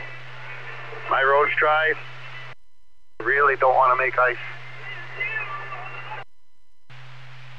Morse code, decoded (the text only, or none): none